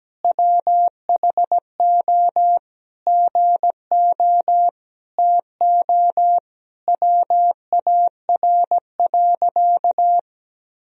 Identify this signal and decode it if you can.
Morse code, heard as a keyed tone, about 17 wpm, 700 Hz. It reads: WHO GO TO WAR.
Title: Sentences from Top 1000 Words - Review Part 2 - 17wpm